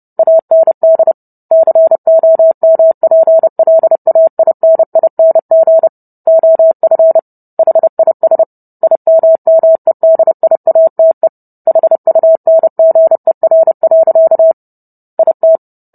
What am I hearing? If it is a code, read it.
Morse code, heard as a keyed tone, about 30 wpm, 650 Hz. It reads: AND COMPLAINING OF HIS IMMEDIATE HUNGER. IT